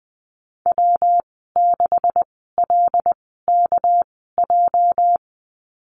Morse code, heard as a keyed tone, about 20 wpm, spaced as for 14 wpm, 700 Hz. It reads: W6LKJ
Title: 213 Lesson 34g - Number 8 - Call Signs 20wpm